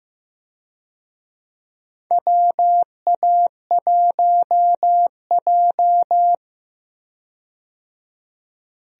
Morse code, heard as a keyed tone, about 15 wpm, 700 Hz. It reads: WA1J
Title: USA Call Signs - 4 Characters - 15wpm